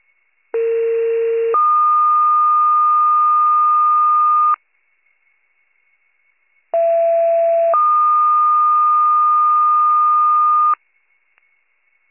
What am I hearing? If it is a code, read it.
Two-tone page, heard as a one-second tone, then a three-second tone, about 460 then 1220 Hz, about 670 then 1220 Hz. It is heard from 0.5 to 10.7 s.